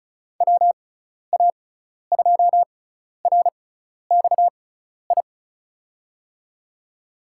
Morse code, heard as a keyed tone, about 36 wpm, spaced as for 12 wpm, 700 Hz. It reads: WA2RXI
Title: USA Call Signs - 6 Characters - 6x Character Spacing - 35wpm